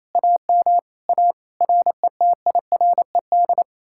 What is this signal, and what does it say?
Morse code, heard as a keyed tone, about 38 wpm, 700 Hz. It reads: AM A RETIRED